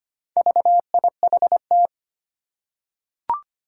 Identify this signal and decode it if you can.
Morse code, heard as a keyed tone, about 25 wpm, 700 Hz. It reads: VIHT
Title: Four Challenging Letter-Number Combinations 25wpm